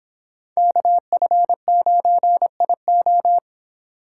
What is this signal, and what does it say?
Morse code, heard as a keyed tone, about 24 wpm, 700 Hz. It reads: KF9IO